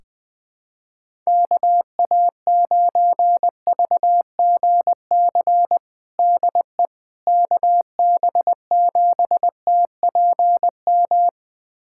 Morse code, heard as a keyed tone, about 20 wpm, 700 Hz. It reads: KA9VGC DE KB7TPM
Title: USA Callsign DE Callsign - 6 Characters - 20wpm